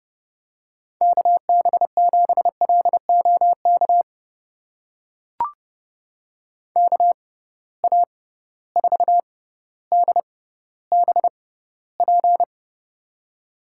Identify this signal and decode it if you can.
Morse code, heard as a keyed tone, about 30 wpm, spaced as for 15 wpm, 700 Hz. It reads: KB7LOK KA4DBP
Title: USA Call Signs - 6 Characters - 6x Character Spacing - 30wpm